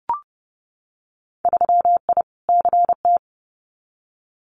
Morse code, heard as a keyed tone, about 30 wpm, 700 Hz. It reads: 3I CT
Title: Winter Field Day Exchanges 30wpm